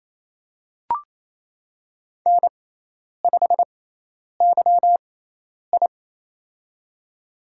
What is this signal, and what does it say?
Morse code, heard as a keyed tone, about 27 wpm, spaced as for 10 wpm, 700 Hz. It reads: N5YI
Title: USA Call Signs - 4 Characters - 6x Character Spacing - 28wpm